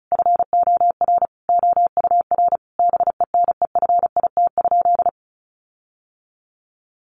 Morse code, heard as a keyed tone, about 34 wpm, 700 Hz. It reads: FOR OUR BENEFIT?
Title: Sentences from Top 1500 Words 35wpm